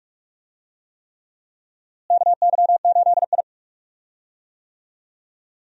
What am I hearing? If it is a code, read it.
Morse code, heard as a keyed tone, about 43 wpm, 700 Hz. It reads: KY8I
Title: USA Call Signs - 4 Characters - Repeated 3x - 45wpm